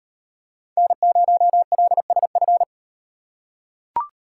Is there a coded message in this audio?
Yes: Morse code from 0.8 to 2.6 s, heard as a keyed tone, about 38 wpm, 700 Hz. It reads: N0LSF